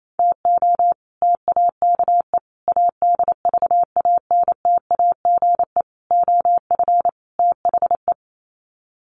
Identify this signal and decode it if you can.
Morse code, heard as a keyed tone, about 28 wpm, 700 Hz. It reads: TO TAKE ADVANTAGE OF THE
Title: Sets of 5 Words 28wpm